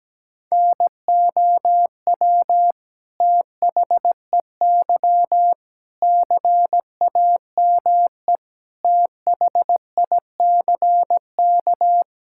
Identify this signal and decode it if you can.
Morse code, heard as a keyed tone, about 17 wpm, 700 Hz. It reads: NOW THEY CAME THICK
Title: Sentences from Top 1600 Words 17wpm